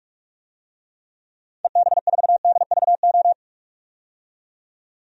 Morse code, heard as a keyed tone, about 44 wpm, 690 Hz. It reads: EB4DVO
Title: DXCC-100 Call Signs - 6 Characters - 45wpm